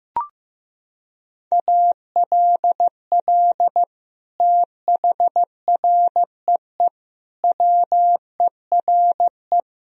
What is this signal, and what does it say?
Morse code, heard as a keyed tone, about 15 wpm, 700 Hz. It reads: ALL THREE WERE
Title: Sentences from Top 1000 Words - Review Part 4 - 15wpm